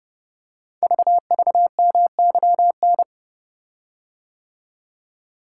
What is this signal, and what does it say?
Morse code, heard as a keyed tone, about 30 wpm, 700 Hz. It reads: VVMYN
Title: Code Groups - 5 Letters - 30wpm